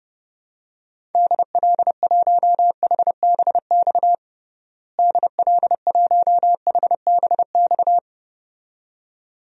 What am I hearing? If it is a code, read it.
Morse code, heard as a keyed tone, about 32 wpm, 700 Hz. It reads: DL1HBX DL1HBX
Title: DXCC-100 Call Signs - 6 Characters - Repeated 2x - 30wpm